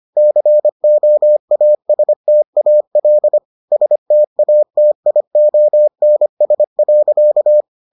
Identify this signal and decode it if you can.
Morse code, heard as a keyed tone, about 25 wpm, 600 Hz. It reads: COASTAL STATIONS.